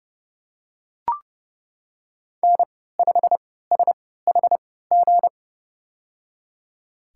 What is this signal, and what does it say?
Morse code, heard as a keyed tone, about 30 wpm, spaced as for 17 wpm, 700 Hz. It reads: N5SHG
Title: USA Call Signs - 3x Character Spacing - 30wpm